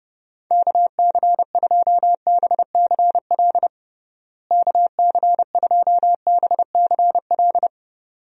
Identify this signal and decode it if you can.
Morse code, heard as a keyed tone, about 30 wpm, 700 Hz. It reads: KC2BCL KC2BCL
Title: USA Call Signs - 6 Characters - Repeated 2x - 30wpm